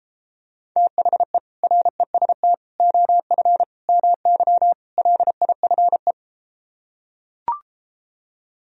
Morse code, heard as a keyed tone, about 33 wpm, 700 Hz. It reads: THE REST OF MY LIFE